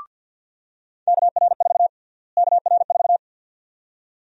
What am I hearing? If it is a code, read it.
Morse code, heard as a keyed tone, about 46 wpm, 700 Hz. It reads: KR4 KR4
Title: USA Call Sign Prefixes - Repeated 2x - 50wpm